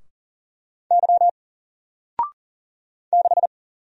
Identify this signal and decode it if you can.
Morse code, heard as a keyed tone, about 40 wpm, 700 Hz. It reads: Y 6
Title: Single Character (Letter:Number:Punctuation:Prosign) 40wpm